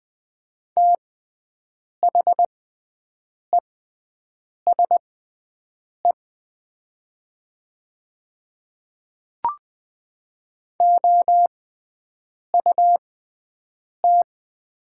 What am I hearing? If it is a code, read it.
Morse code, heard as a keyed tone, about 20 wpm, spaced as for 7 wpm, 700 Hz. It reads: THESE OUT
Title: 6x Character Spacing - Top 100 Words - 20wpm